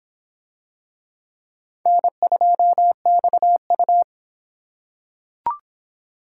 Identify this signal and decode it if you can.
Morse code, heard as a keyed tone, about 26 wpm, 700 Hz. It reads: N2XU